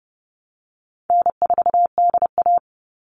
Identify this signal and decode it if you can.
Morse code, heard as a keyed tone, about 30 wpm, 700 Hz. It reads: N4DA
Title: NA SOTA Chaser Callsigns 30wpm